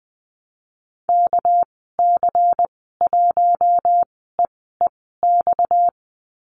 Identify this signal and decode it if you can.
Morse code, heard as a keyed tone, about 20 wpm, spaced as for 14 wpm, 700 Hz. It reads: KC1EEX
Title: 245 Lesson 39g - Letter z - Call Signs 20wpm